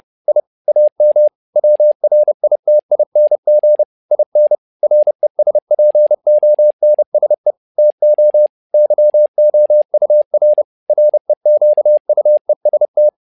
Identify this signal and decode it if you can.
Morse code, heard as a keyed tone, about 30 wpm, 600 Hz. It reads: I AM WRITING IN RESPONSE TO YOUR REQUEST
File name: learning center / tower trouble / tower trouble 30wpm